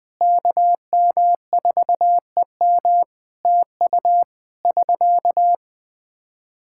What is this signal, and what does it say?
Morse code, heard as a keyed tone, about 20 wpm, 700 Hz. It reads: KM4EM TU <SK>
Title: Practice QSO - Exchange 4 - Ending - 20wpm